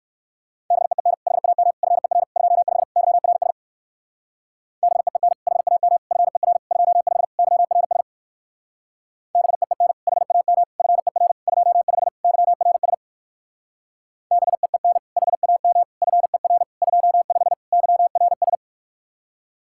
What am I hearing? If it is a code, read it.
Morse code, heard as a keyed tone, about 59 wpm, 700 Hz. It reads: BEEN HAM FER 25 YRS BEEN HAM FER 25 YRS BEEN HAM FER 25 YRS BEEN HAM FER 25 YRS